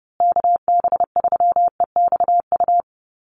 Morse code, heard as a keyed tone, about 30 wpm, 700 Hz. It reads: KB3EXU
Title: USA Call Signs Repeated 2x 30wpm